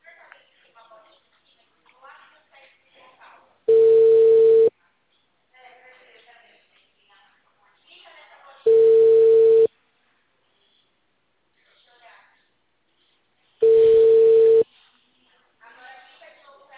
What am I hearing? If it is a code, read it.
Ringback tone (China, 450 Hz), with speech.